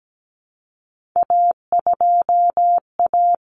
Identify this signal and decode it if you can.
Morse code, heard as a keyed tone, about 17 wpm, 700 Hz. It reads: A2A
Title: Common Cut Numbers - 3-Digit 17wpm